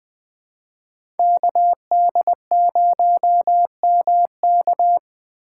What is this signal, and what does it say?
Morse code, heard as a keyed tone, about 20 wpm, 700 Hz. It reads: KD0MK
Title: USA Call Signs - 5 Characters - 3x Character Spacing - 20wpm